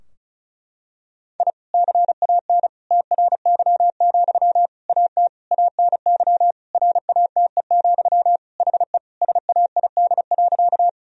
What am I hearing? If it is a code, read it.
Morse code, heard as a keyed tone, about 36 wpm, 700 Hz. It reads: I CAN TRY, AT ANY RATE, HE SAID.
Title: Sentences from Top 1000 Words - Review Part 4 - 35wpm